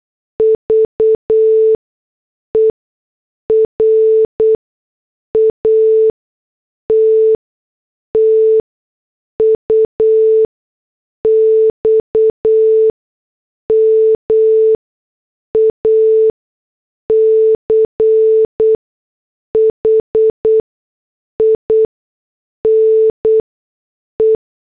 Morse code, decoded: VER AT TUX MACHINE